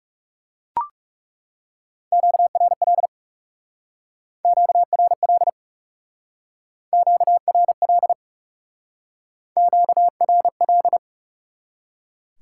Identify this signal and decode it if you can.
Morse code, heard as a keyed tone, about 38 wpm, 700 Hz. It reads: QRL QRL QRL QRL